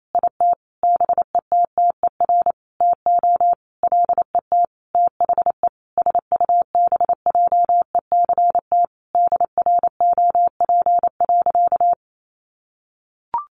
Morse code, heard as a keyed tone, about 28 wpm, 700 Hz. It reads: IT BETTER TO LET THE SUBJECT DROP.
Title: Sentences from Top 1400 Words 28wpm